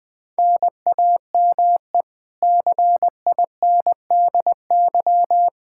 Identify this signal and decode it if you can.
Morse code, heard as a keyed tone, about 20 wpm, 700 Hz. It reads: NAME CINDY
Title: K1USN SST QSO Practice 20wpm